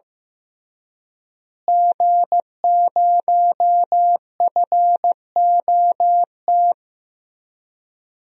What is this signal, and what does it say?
Morse code, heard as a keyed tone, about 15 wpm, 700 Hz. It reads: G0FOT
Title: DXCC-100 Call Signs - 5 Characters - Repeated 3x - 15wpm